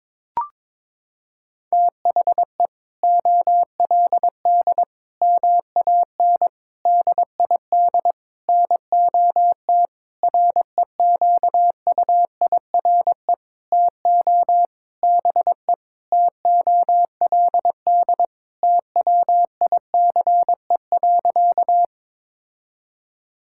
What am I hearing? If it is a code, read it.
Morse code, heard as a keyed tone, about 22 wpm, 700 Hz. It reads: THE OLD MAN DID NOT REQUIRE TO BE TOLD TWICE.
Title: Sentences from Top 1600 Words 22wpm